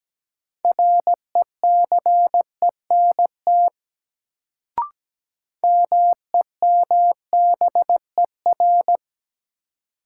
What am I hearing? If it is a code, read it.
Morse code, heard as a keyed tone, about 17 wpm, 700 Hz. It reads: RECENT MEMBER
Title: Sets of 3 Words - New Word Drill - 17wpm